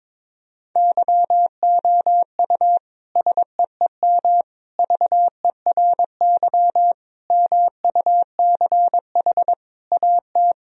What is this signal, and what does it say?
Morse code, heard as a keyed tone, about 22 wpm, 700 Hz. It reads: YOU SEEM VERY MUCH AT
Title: Sentences from Top 1000 Words - Review Part 1 - 22wpm